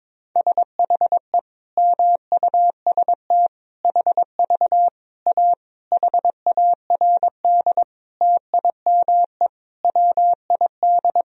Morse code, heard as a keyed tone, about 22 wpm, 700 Hz. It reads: SHE MUST HV A HARD TIME WID
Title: Sentences from Top 300 Words and Abbreviations 22wpm